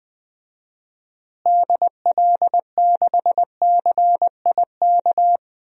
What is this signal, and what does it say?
Morse code, heard as a keyed tone, about 20 wpm, 700 Hz. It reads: DL6CIK
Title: DXCC-100 Call Signs - 6 Characters - Repeated 3x - 20wpm